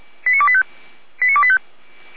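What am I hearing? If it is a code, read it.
Five-tone selective call (ZVEI), heard as quick runs of five tones, tones of about 70 ms. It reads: 89285 89285
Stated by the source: ZVEI five-tone selective call 89025, received by SDR.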